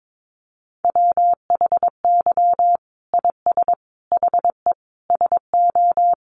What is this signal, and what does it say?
Morse code, heard as a keyed tone, about 22 wpm, 700 Hz. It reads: WHY IS HE SO
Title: Sentences from Top 1200 Words 22wpm